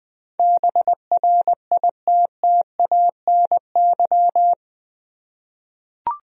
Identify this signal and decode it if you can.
Morse code, heard as a keyed tone, about 20 wpm, 700 Hz. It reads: BRITTANY
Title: USA Common Names - 20wpm